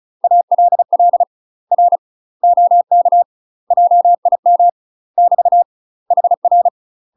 Morse code, heard as a keyed tone, about 34 wpm, 700 Hz. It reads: ALL R OK JIM <BT> HR
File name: UK MASTER QSOs TRAINING / 35WPM / UK MASTER QSOs 0003